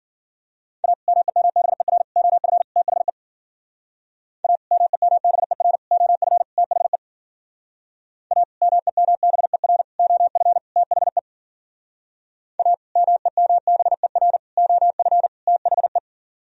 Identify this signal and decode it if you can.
Morse code, heard as a keyed tone, about 50 wpm, 700 Hz. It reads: A MEMBER OF THE A MEMBER OF THE A MEMBER OF THE A MEMBER OF THE